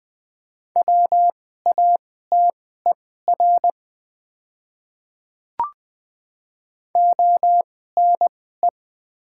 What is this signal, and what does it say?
Morse code, heard as a keyed tone, about 20 wpm, spaced as for 14 wpm, 690 Hz. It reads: WATER ONE
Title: 2x Character Spacing - Top 100 Words - 20wpm